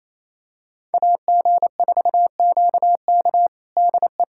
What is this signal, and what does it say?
Morse code, heard as a keyed tone, about 28 wpm, 700 Hz. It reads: AG4QK DE
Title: USA Callsign DE Callsign - 5 Characters - 28wpm